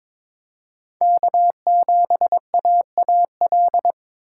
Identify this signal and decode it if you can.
Morse code, heard as a keyed tone, about 26 wpm, 700 Hz. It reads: K7AAL
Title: Speed Racing - USA 1x3 Call Signs 22wpm